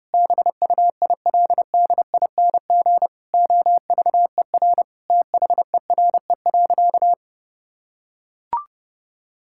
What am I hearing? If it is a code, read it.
Morse code, heard as a keyed tone, about 30 wpm, 700 Hz. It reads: BUILDING OVER THERE.
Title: Sentences from Top 1000 Words - Review Part 5 - 30wpm